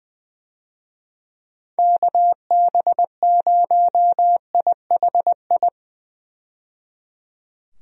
Morse code, heard as a keyed tone, about 20 wpm, 700 Hz. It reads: KB0IHI
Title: USA Call Signs - 6 Characters - 20wpm